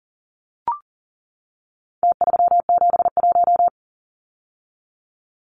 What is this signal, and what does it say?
Morse code, heard as a keyed tone, about 40 wpm, 700 Hz. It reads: T371
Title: Numbers - Contest Style - 4-Digit 40wpm